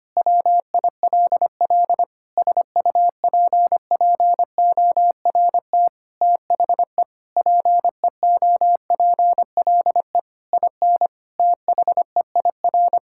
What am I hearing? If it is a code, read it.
Morse code, heard as a keyed tone, about 25 wpm, 700 Hz. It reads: WILL SUPPORT THE PEOPLE IN THEIR
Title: Sentences from Top 1000 Words - Review Part 5 - 25wpm